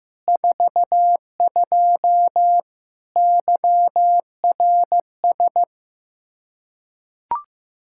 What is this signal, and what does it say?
Morse code, heard as a keyed tone, about 16 wpm, 700 Hz. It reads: 42 YRS
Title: Practice QSO - Exchange 3 - More Info - 15wpm